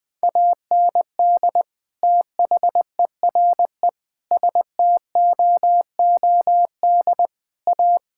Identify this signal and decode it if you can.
Morse code, heard as a keyed tone, about 20 wpm, 700 Hz. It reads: AND THERE STOOD A
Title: Sentences with 2-Word Phrases - 20wpm